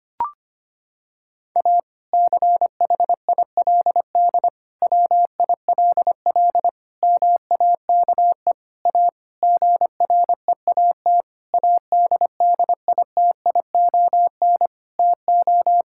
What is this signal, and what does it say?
Morse code, heard as a keyed tone, about 25 wpm, 700 Hz. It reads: A CHILD WILL MAKE A GREAT ADDITION TO
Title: Sentences from Top 1000 Words - Review Part 6 - 25wpm